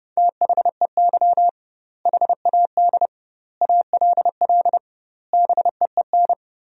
Morse code, heard as a keyed tone, about 30 wpm, 700 Hz. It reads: THEY HAD ALL BEEN
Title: 2x Word-Spacing - Sentences from Top 200 Words - 30wpm